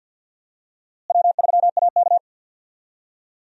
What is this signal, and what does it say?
Morse code, heard as a keyed tone, about 52 wpm, 700 Hz. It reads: W3AK
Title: USA Call Signs - 4 Characters - Repeated 2x - 50wpm